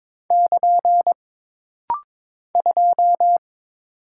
Morse code, heard as a keyed tone, about 22 wpm, 700 Hz. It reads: <KN> 2
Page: This rapid-fire timing is geared to push your ICR (Instant Character Recognition) skills.